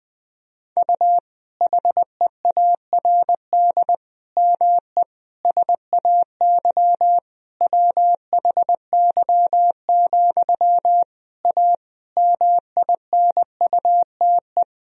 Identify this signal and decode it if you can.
Morse code, heard as a keyed tone, about 20 wpm, 700 Hz. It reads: U HEARD ME SAY WHY, A MINUTE
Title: Sentences from Top 500 Words and Abbreviations 20wpm